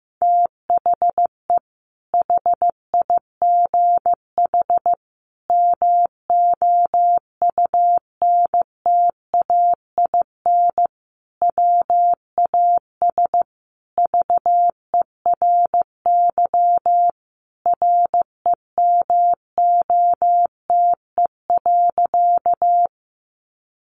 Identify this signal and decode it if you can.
Morse code, heard as a keyed tone, about 15 wpm, 700 Hz. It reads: THE HIGH MOUNTAIN WAS VERY REMOTE.
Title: Sentences from Top 1300 Words 15wpm